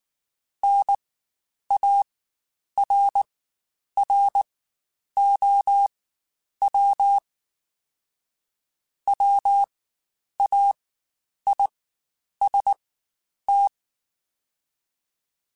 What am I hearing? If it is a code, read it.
Morse code, heard as a keyed tone, about 19 wpm, spaced as for 9 wpm, 800 Hz. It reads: NARROW WAIST
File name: Spaced / Chuckles Spaced-mp3s / MiddleAge sp @18